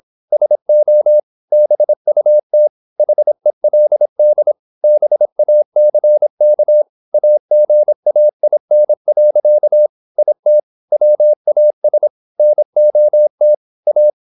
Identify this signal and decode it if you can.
Morse code, heard as a keyed tone, about 26 wpm, 600 Hz. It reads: SO BUT HELD BACK AGAIN. IT WAS NOT A
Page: Never 26wpm | CW med Gnister